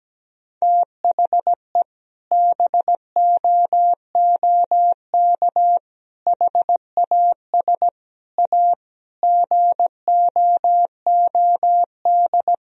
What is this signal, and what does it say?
Morse code, heard as a keyed tone, about 17 wpm, 700 Hz. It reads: THE BOOK HAS A GOOD
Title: Sentences from Top 1000 Words - Review Part 3 - 17wpm